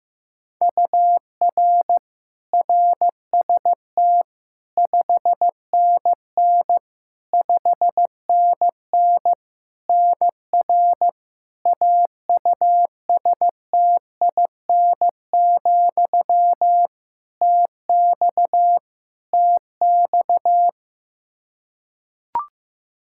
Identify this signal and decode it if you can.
Morse code, heard as a keyed tone, about 15 wpm, 700 Hz. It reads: UR RST 5NN 5NN NR AUSTIN, TX TX
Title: Practice QSO - Exchange 1 - Beginning - 15wpm